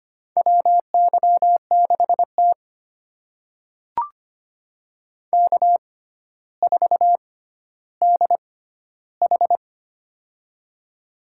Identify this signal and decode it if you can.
Morse code, heard as a keyed tone, about 25 wpm, spaced as for 13 wpm, 700 Hz. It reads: WY6T K4DH
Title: USA Call Signs - 4 Characters - 6x Character Spacing - 25wpm